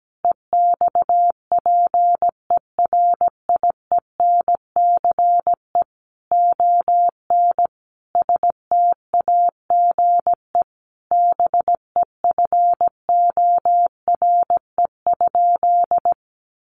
Morse code, heard as a keyed tone, about 17 wpm, 700 Hz. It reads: EXPERIENCE ON STAGE BEFORE?
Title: Sentences from Top 1200 Words 17wpm